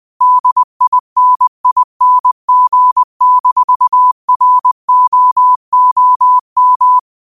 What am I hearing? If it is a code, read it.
Morse code, heard as a keyed tone, about 20 wpm, 1000 Hz. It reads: DINING-ROOM
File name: qotdfiles / 20wpm